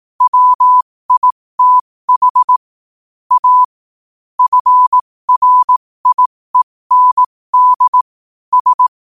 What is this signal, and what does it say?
Morse code, heard as a keyed tone, about 18 wpm, spaced as for 15 wpm, 1000 Hz. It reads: WITH A FRIEND S
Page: 15 WPM morse code quotes for Tue, 23 Sep 2025.